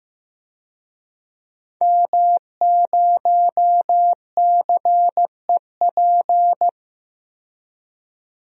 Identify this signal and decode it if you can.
Morse code, heard as a keyed tone, about 15 wpm, 700 Hz. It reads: M0CEP
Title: DXCC-100 Call Signs - 5 Characters - 15wpm